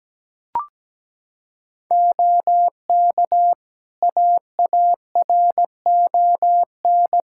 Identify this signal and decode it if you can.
Morse code, heard as a keyed tone, about 17 wpm, 700 Hz. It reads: OK AARON
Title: K1USN SST QSO Practice 17wpm